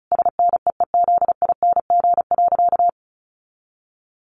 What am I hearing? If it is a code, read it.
Morse code, heard as a keyed tone, about 34 wpm, 700 Hz. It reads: SNEEZING.
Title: Easy Phrases 35wpm